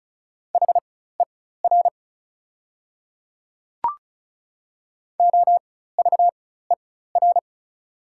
Morse code, heard as a keyed tone, about 34 wpm, spaced as for 16 wpm, 700 Hz. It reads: HER OVER
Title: 4x Character Spacing - Top 100 Words - 35wpm